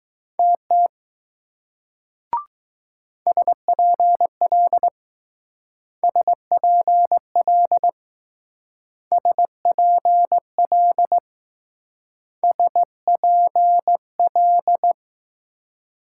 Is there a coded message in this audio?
Yes: Morse code from 0.4 to 14.9 s, heard as a keyed tone, about 20 wpm, 700 Hz. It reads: TT SPL SPL SPL SPL